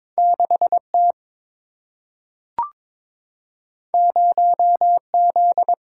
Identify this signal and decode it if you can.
Morse code, heard as a keyed tone, about 22 wpm, 700 Hz. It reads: 6 T 0 Z
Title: Single-Letter Single-Number 22wpm